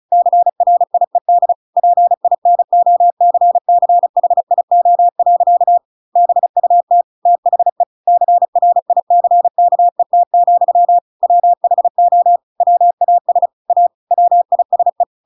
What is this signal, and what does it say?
Morse code, heard as a keyed tone, about 34 wpm, 700 Hz. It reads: CRIED PINOCCHIO. BUT THE CRICKET, WHO WAS A WISE